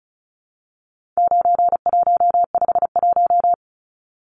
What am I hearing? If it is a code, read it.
Morse code, heard as a keyed tone, about 34 wpm, 700 Hz. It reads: 9151
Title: Numbers - 4-Digit 35wpm